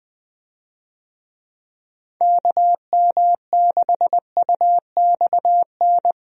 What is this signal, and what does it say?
Morse code, heard as a keyed tone, about 20 wpm, 700 Hz. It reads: KM6UXN
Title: USA Call Signs - 6 Characters - 20wpm